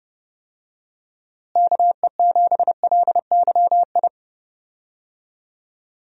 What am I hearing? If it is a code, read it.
Morse code, heard as a keyed tone, about 30 wpm, 700 Hz. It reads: KE7LYI
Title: USA Call Signs - 6 Characters - Repeated 2x - 30wpm